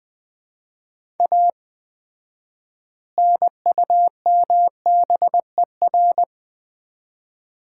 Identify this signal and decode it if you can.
Morse code, heard as a keyed tone, about 20 wpm, 700 Hz. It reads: A NUMBER
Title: 4x Word-Spacing - Sets of 3 Words - 20wpm